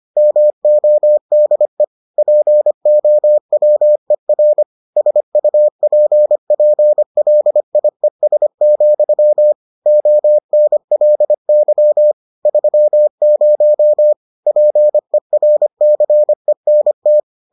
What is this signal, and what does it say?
Morse code, heard as a keyed tone, about 25 wpm, 600 Hz. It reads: MODE POWER SUPPLIES, ONLY 30 PERCENT